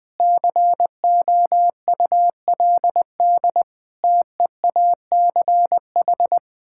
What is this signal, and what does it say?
Morse code, heard as a keyed tone, about 20 wpm, 700 Hz. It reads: COULD TEACH
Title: Sentences from Top 1000 Words - Review Part 6 - 20wpm